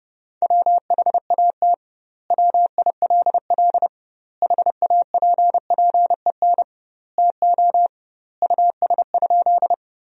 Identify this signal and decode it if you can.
Morse code, heard as a keyed tone, about 30 wpm, 700 Hz. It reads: WHAT WILL HAPPEN TO US?
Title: 2x Word-Spacing - Sentences from Top 400 Words - 30wpm